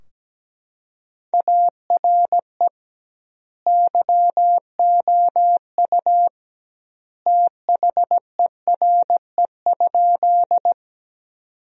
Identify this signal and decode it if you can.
Morse code, heard as a keyed tone, about 17 wpm, 700 Hz. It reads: ARE YOU THERE?
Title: 2x Word-Spacing - Sentences from Top 100 Words - 17wpm